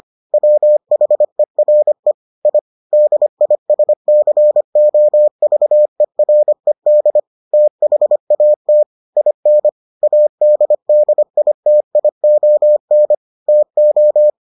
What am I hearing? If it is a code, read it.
Morse code, heard as a keyed tone, about 25 wpm, 600 Hz. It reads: WHERE I DISCOVERED THAT IN ADDITION TO